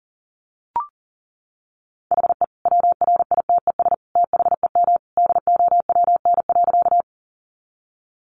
Morse code, heard as a keyed tone, about 38 wpm, 700 Hz. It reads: HE WRITES THEM DOWN.
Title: Additional Easy Phrases 40wpm